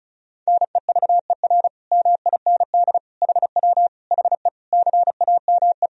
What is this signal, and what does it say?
Morse code, heard as a keyed tone, about 36 wpm, 700 Hz. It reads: NEVER MIND HW HE CAME
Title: Sentences from Top 500 Words and Abbreviations 35wpm